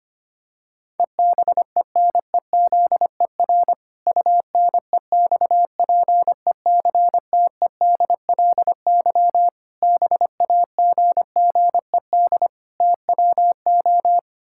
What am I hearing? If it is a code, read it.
Morse code, heard as a keyed tone, about 25 wpm, 700 Hz. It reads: EBENEZER UNEXPECTEDLY BAGGED TWO
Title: Pangram Challenge v2 - 25wpm